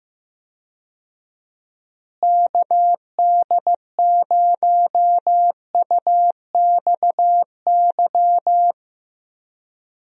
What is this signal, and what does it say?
Morse code, heard as a keyed tone, about 15 wpm, 700 Hz. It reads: KD0UXY
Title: USA Call Signs - 6 Characters - Repeated 3x - 15wpm